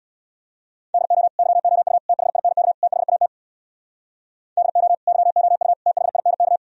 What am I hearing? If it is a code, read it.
Morse code, heard as a keyed tone, about 64 wpm, 700 Hz. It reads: DO YOU THINK THAT DO YOU THINK